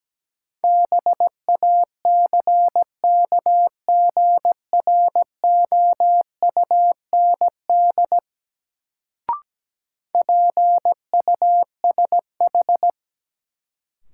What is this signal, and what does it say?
Morse code, heard as a keyed tone, about 17 wpm, 700 Hz. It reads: BACKGROUND PUSH
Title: Sentences from Top 1300 Words - New Word Drill - 17wpm